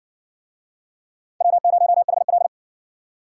Morse code, heard as a keyed tone, about 58 wpm, 700 Hz. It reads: W0HL